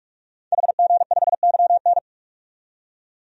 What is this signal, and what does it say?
Morse code, heard as a keyed tone, about 44 wpm, 690 Hz. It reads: HG5YN
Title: DXCC-100 Call Signs - 5 Characters - 45wpm